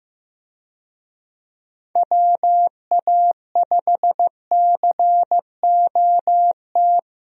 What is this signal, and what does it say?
Morse code, heard as a keyed tone, about 15 wpm, 700 Hz. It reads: WA5COT
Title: USA Call Signs - 6 Characters - Repeated 3x - 15wpm